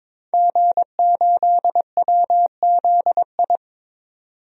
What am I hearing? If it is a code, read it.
Morse code, heard as a keyed tone, about 22 wpm, 700 Hz. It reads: G8WZI
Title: DXCC-100 Call Signs - 5 Characters - 22wpm